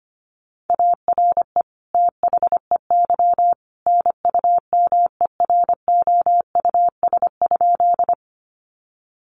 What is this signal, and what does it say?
Morse code, heard as a keyed tone, about 25 wpm, 700 Hz. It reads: ARE THEY NUMEROUS?
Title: Sentences from Top 1400 Words 25wpm